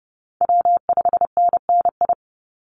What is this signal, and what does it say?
Morse code, heard as a keyed tone, about 30 wpm, 700 Hz. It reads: W5NNI
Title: USA Call Signs 30wpm